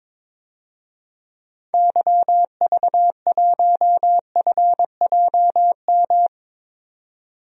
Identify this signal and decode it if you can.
Morse code, heard as a keyed tone, about 22 wpm, 700 Hz. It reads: YV1FJM
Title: DXCC-100 Call Signs - 6 Characters - 22wpm